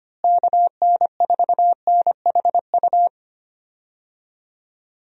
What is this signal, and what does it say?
Morse code, heard as a keyed tone, about 25 wpm, 700 Hz. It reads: KN4NHU
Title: USA Call Signs - 6 Characters - Repeated 2x - 25wpm